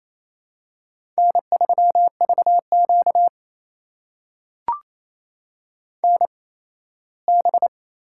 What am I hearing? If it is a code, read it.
Morse code, heard as a keyed tone, about 28 wpm, spaced as for 17 wpm, 700 Hz. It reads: N3VQ NB
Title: USA Call Signs - 4 Characters - 8x Character Spacing - 28wpm